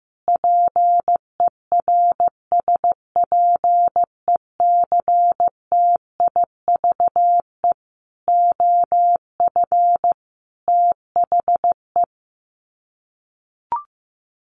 Morse code, heard as a keyed tone, about 15 wpm, 700 Hz. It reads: PERSPECTIVE OF THE
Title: Sets of 5 Words Encore2 15wpm